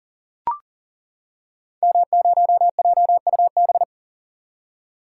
Morse code, heard as a keyed tone, about 40 wpm, 700 Hz. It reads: M0JUB
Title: Code Groups - 5 Letter:Numbers - 40wpm